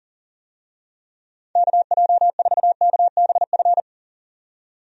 Morse code, heard as a keyed tone, about 40 wpm, 700 Hz. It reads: KJ4KBF
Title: USA Call Signs - 6 Characters - 40wpm